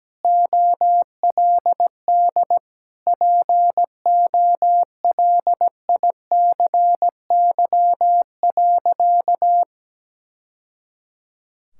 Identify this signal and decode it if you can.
Morse code, heard as a keyed tone, about 17 wpm, 700 Hz. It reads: OLD POLICY.
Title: Sentences from Top 1000 Words - Review Part 5 - 17wpm